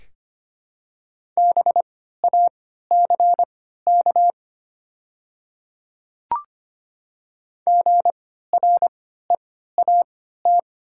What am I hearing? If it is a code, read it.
Morse code, heard as a keyed tone, about 25 wpm, spaced as for 14 wpm, 690 Hz. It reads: BACK GREAT
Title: 3x Character Spacing - Top 200 Words - 25wpm